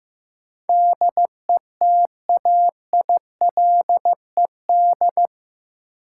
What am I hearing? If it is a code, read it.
Morse code, heard as a keyed tone, about 15 wpm, 700 Hz. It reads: DETAILED
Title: Sentences from Top 1500 Words - New Word Drill Alt - 15wpm